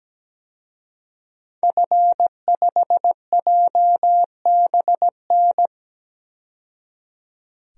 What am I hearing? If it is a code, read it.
Morse code, heard as a keyed tone, about 17 wpm, 700 Hz. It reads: F5JBN
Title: DXCC-100 Call Signs - 5 Characters - 17wpm